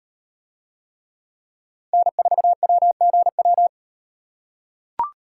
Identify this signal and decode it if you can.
Morse code, heard as a keyed tone, about 38 wpm, 700 Hz. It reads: N4WGW